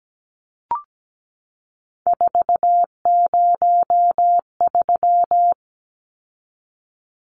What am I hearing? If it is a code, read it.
Morse code, heard as a keyed tone, about 17 wpm, 700 Hz. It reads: 403
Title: Numbers - 3-Digit 17wpm